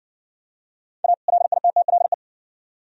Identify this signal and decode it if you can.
Morse code, heard as a keyed tone, about 57 wpm, 700 Hz. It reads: A LITTLE